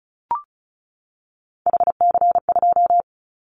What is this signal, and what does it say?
Morse code, heard as a keyed tone, about 34 wpm, 700 Hz. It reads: HC2
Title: Callsign Prefixes for Top 100 DXCC - 35wpm